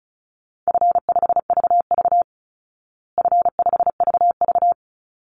Morse code, heard as a keyed tone, about 34 wpm, 700 Hz. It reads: F5VV F5VV
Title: Easy Call Signs Repeated 2x - 35wpm